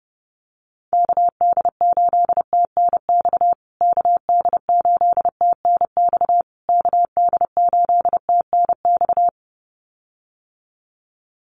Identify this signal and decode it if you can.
Morse code, heard as a keyed tone, about 28 wpm, 700 Hz. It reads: KD8TNX KD8TNX KD8TNX
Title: USA Call Signs Repeated 3x 30wpm